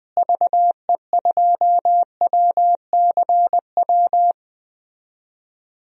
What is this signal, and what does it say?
Morse code, heard as a keyed tone, about 21 wpm, 700 Hz. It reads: VE2WCW
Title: DXCC-100 Call Signs - 6 Characters - Repeated 3x - 20wpm